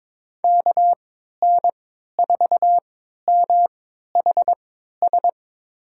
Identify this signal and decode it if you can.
Morse code, heard as a keyed tone, about 22 wpm, spaced as for 13 wpm, 700 Hz. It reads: KN4MHS
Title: USA Call Signs - 6 Characters - 3x Character Spacing - 22wpm